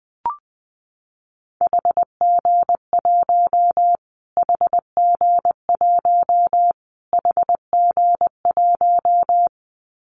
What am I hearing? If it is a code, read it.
Morse code, heard as a keyed tone, about 19 wpm, 700 Hz. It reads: HG1 HG1 HG1
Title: Callsign Prefixes for Top 25 DXCC - Repeated 3x - 20wpm